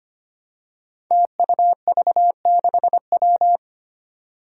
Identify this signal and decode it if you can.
Morse code, heard as a keyed tone, about 25 wpm, 700 Hz. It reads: TUV6W
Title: Five Challenging Letter-Number Combinations 25wpm